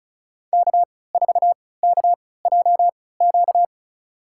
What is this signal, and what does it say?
Morse code, heard as a keyed tone, about 34 wpm, spaced as for 20 wpm, 690 Hz. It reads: K4KJQ
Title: USA Call Signs - 5 Characters - 3x Character Spacing - 35wpm